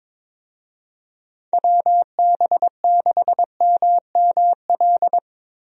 Morse code, heard as a keyed tone, about 21 wpm, 700 Hz. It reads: WB6MML
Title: USA Call Signs - 6 Characters - Repeated 3x - 22wpm